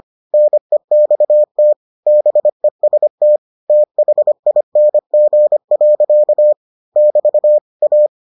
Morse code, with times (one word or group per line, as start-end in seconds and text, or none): 0.3-1.7 NEXT
2.1-3.4 BEST
3.7-6.5 THING.
7.0-7.6 <BT>
7.8-8.1 A